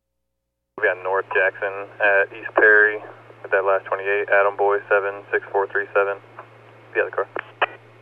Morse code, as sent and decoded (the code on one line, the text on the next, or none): none